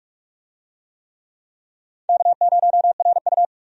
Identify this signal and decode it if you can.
Morse code, heard as a keyed tone, about 46 wpm, 700 Hz. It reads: K0RU
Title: USA Call Signs - 4 Characters - 45wpm